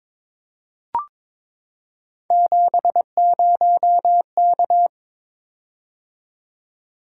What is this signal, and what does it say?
Morse code, heard as a keyed tone, about 23 wpm, 700 Hz. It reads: 70K
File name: practice / 22wpm / Three Letter-Number Combinations - 22wpm